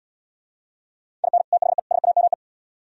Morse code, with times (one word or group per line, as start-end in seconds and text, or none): 1.2-1.4 IN
1.5-1.8 THE
1.9-2.3 SAME